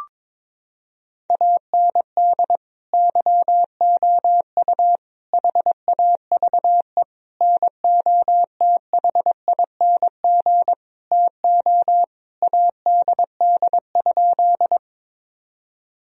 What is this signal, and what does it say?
Morse code, heard as a keyed tone, about 21 wpm, 700 Hz. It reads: AND YOU HAVE NOTHING TO ADD?
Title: Sentences from Top 1000 Words - Review Part 1 - 22wpm